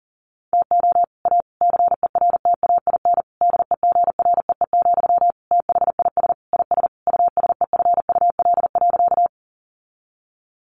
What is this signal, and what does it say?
Morse code, heard as a keyed tone, about 40 wpm, 700 Hz. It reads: TO A CERTAIN DEGREE, THIS IS USEFUL.
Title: Sentences from Top 1500 Words 40wpm